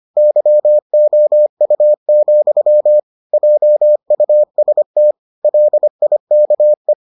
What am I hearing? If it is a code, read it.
Morse code, heard as a keyed tone, about 25 wpm, 600 Hz. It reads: YOU, JUST LIKE